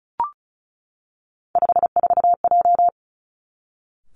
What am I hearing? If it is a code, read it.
Morse code, heard as a keyed tone, about 36 wpm, 700 Hz. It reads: 54J
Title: Three Challenging Letter-Number Combinations 35wpm